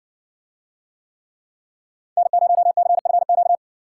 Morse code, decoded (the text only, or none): N0CFQ